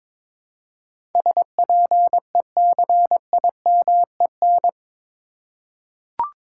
Morse code, heard as a keyed tone, about 22 wpm, 700 Hz. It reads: SPECIMEN
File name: practice / 22wpm / Eight Letter Words - 22wpm